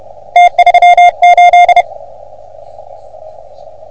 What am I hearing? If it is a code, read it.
Morse code, heard as a keyed tone, about 31 wpm, 710 Hz. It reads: T38
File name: OM3PA PILUP CUC 388